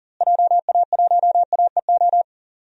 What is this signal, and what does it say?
Morse code, heard as a keyed tone, about 37 wpm, 700 Hz. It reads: JA1AEO
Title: DXCC-100 Call Signs - 6 Characters - Repeated 2x - 40wpm